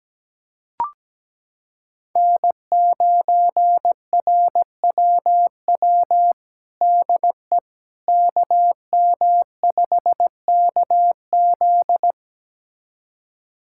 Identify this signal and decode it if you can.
Morse code, heard as a keyed tone, about 17 wpm, 700 Hz. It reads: N9RWW DE KM5KZ
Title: USA Callsign DE Callsign - 5 Characters - 17wpm